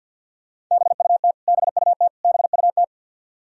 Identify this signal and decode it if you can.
Morse code, heard as a keyed tone, about 51 wpm, 700 Hz. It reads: BUT BUT BUT
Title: Top 100 Words Repeated 50wpm